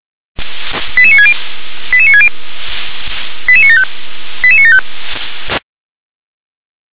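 Five-tone selective call (ZVEI), heard as quick runs of five tones, tones of about 70 ms. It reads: 88960 88960 88975 88975
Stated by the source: ZVEI five-tone call received by SDR.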